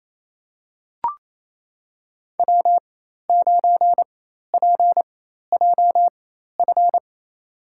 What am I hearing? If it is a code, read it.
Morse code, heard as a keyed tone, about 28 wpm, spaced as for 13 wpm, 700 Hz. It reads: W9PJF
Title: USA Call Signs - 5 Characters - 4x Character Spacing - 28wpm